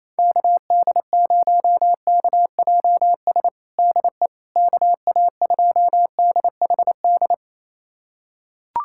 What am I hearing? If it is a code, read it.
Morse code, heard as a keyed tone, about 28 wpm, 700 Hz. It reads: KD0KJS DE KA2DHD
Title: USA Callsign DE Callsign - 6 Characters - 28wpm